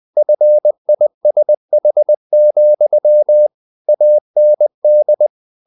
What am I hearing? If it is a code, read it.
Morse code, heard as a keyed tone, about 20 wpm, 600 Hz. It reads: FISH, AND
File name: the old man and the sea / the old man and the sea part 1 20wpm